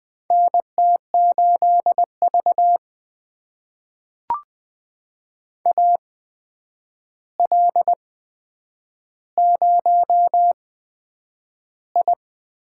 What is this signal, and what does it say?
Morse code, heard as a keyed tone, about 20 wpm, spaced as for 9 wpm, 700 Hz. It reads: NT8V AL0I
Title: USA Call Signs - 4 Characters - 8x Character Spacing - 20wpm